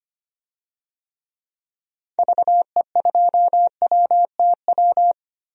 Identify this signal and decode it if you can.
Morse code, heard as a keyed tone, about 25 wpm, 700 Hz. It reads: VE2WTW
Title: DXCC-100 Call Signs - 6 Characters - Repeated 2x - 25wpm